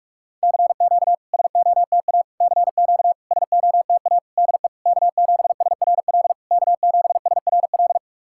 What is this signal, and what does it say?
Morse code, heard as a keyed tone, about 46 wpm, 690 Hz. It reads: CQ SOTA CQ SOTA DE K7SRL K7SRL
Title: SOTA QSO Practice 45wpm